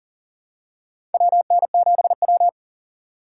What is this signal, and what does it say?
Morse code, heard as a keyed tone, about 40 wpm, 700 Hz. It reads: WN7W
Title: USA Call Signs - 4 Characters - 6x Character Spacing - 40wpm